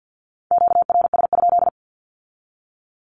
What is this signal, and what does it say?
Morse code, heard as a keyed tone, about 50 wpm, 700 Hz. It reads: QRS?
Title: Q-Codes with Spoken Meaning - 50wpm